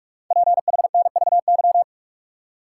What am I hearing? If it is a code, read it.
Morse code, heard as a keyed tone, about 44 wpm, 700 Hz. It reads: PHNVY
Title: Code Groups - 5 Letters - 45wpm